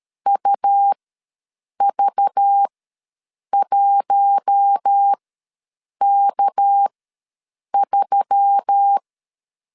Morse code, heard as a keyed tone, about 13 wpm, spaced as for 7 wpm, 800 Hz. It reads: UV1K3
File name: LEZIONI / 6 PKX0 / 6 PK ws 12wpm 3116